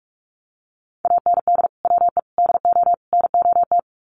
Morse code, heard as a keyed tone, about 44 wpm, 700 Hz. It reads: AND WE DO NOT
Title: Sets of 4 Words 45wpm